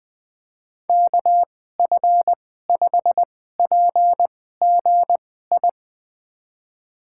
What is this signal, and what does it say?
Morse code, heard as a keyed tone, about 20 wpm, spaced as for 14 wpm, 700 Hz. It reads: KF5PGI
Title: USA Call Signs - 6 Characters - 2x Character Spacing - 20wpm